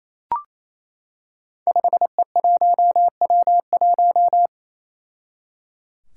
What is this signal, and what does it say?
Morse code, heard as a keyed tone, about 29 wpm, 700 Hz. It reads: 5E1W1
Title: Five Challenging Letter-Number Combinations 28wpm